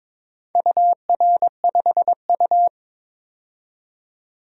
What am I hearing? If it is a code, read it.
Morse code, heard as a keyed tone, about 22 wpm, 700 Hz. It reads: UR5U
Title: DXCC-100 Call Signs - 4 Characters - 22wpm